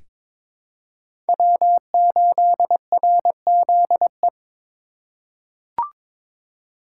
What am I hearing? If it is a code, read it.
Morse code, heard as a keyed tone, about 22 wpm, 700 Hz. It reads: W8RZE
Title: USA Call Signs - 5 Characters - 3x Character Spacing - 22wpm